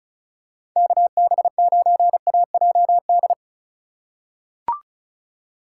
Morse code, heard as a keyed tone, about 36 wpm, 700 Hz. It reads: KB9AJD
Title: USA Call Signs - 3x Character Spacing - 35wpm